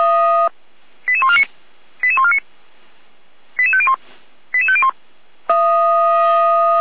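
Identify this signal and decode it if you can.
Five-tone selective call (ZVEI), heard as quick runs of five tones, tones of about 70 ms. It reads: 88149 88149 88501 88501
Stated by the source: ZVEI five-tone selective call received on an SDR, from the Feuerwehr Orlamünde fire brigade audio server.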